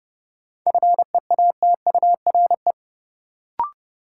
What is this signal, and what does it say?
Morse code, heard as a keyed tone, about 32 wpm, 700 Hz. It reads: FEATURE
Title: Sentences from Top 800 Words - New Word Drill - 30wpm